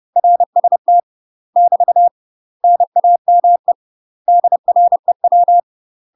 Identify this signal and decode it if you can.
Morse code, heard as a keyed tone, about 30 wpm, 700 Hz. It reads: RST <BT> NAME DREW